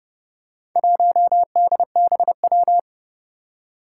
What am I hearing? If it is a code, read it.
Morse code, heard as a keyed tone, about 28 wpm, 700 Hz. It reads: 1DBW
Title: Four Challenging Letter-Number Combinations 30wpm